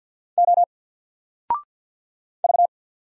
Morse code, heard as a keyed tone, about 50 wpm, 700 Hz. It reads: O V
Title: Single Letter-Number 50wpm